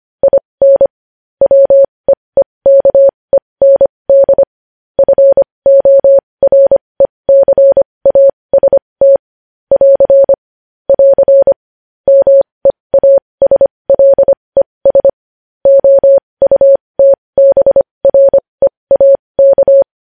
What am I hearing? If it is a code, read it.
Morse code, heard as a keyed tone, about 25 wpm, spaced as for 20 wpm, 550 Hz. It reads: IN WEEKEND FORECAST <AR> <AR> MEASLES OUTBREAK
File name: podcast / News headlines-1x-25x20